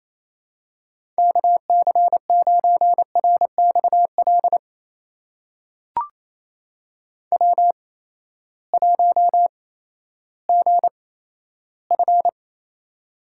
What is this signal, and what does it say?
Morse code, heard as a keyed tone, about 28 wpm, spaced as for 14 wpm, 700 Hz. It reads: KC9RXL W1GF
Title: USA Call Signs - 8x Character Spacing - 28wpm